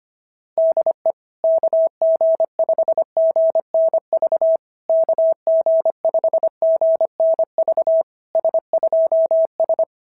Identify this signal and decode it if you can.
Morse code, heard as a keyed tone, about 22 wpm, 650 Hz. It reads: DE KG5GNV KG5GNV S2S
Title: SOTA QSO Practice 25wpm